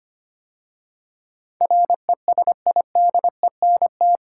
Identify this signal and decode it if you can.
Morse code, heard as a keyed tone, about 25 wpm, 700 Hz. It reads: RESIDENT
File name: 8 Letter Words - First 3 Letters Given - 25wpm